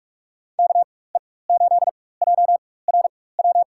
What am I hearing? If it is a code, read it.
Morse code, heard as a keyed tone, about 43 wpm, spaced as for 21 wpm, 700 Hz. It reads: KE8JRW
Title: USA Call Signs - 6 Characters - 4x Character Spacing - 45wpm